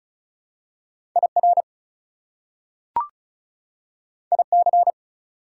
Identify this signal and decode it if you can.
Morse code, heard as a keyed tone, about 36 wpm, 700 Hz. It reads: IR IC
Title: Prefixes and Suffixes 35wpm